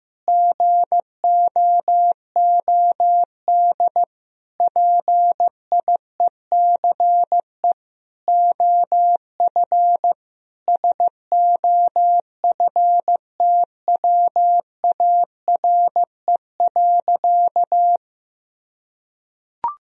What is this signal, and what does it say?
Morse code, heard as a keyed tone, about 16 wpm, 700 Hz. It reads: GOOD PIECE OF SOFTWARE.
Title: Sentences from Top 1000 Words - Review Part 3 - 15wpm